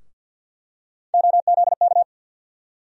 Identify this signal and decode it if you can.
Morse code, heard as a keyed tone, about 52 wpm, 700 Hz. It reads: OZK
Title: USA Call Sign Postfixes - 3 Letters 50wpm